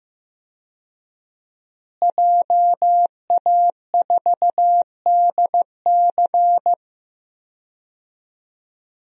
Morse code, heard as a keyed tone, about 15 wpm, 700 Hz. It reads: JA4DC
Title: DXCC-100 Call Signs - 5 Characters - Repeated 2x - 15wpm